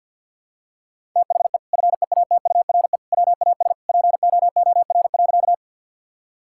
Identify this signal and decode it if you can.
Morse code, heard as a keyed tone, about 53 wpm, 700 Hz. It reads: THE FEATURE WAS POOR.
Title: Sentences from Top 1000 Words - Review Part 5 - 50wpm